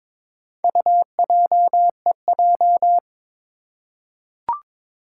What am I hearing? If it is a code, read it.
Morse code, heard as a keyed tone, about 22 wpm, 700 Hz. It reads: UJEJ
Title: Four Challenging Letter-Number Combinations 22wpm